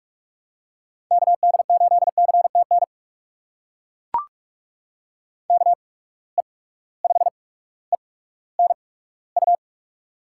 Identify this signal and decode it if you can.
Morse code, heard as a keyed tone, about 42 wpm, spaced as for 19 wpm, 700 Hz. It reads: KD8CTN KE5ENU
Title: USA Call Signs - 6 Characters - 8x Character Spacing - 45wpm